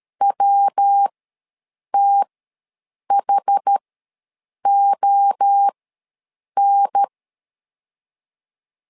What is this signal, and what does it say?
Morse code, heard as a keyed tone, about 13 wpm, spaced as for 7 wpm, 800 Hz. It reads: WTHON
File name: LEZIONI / 3 MWRO / 3 MWRO ws 12wpm 1914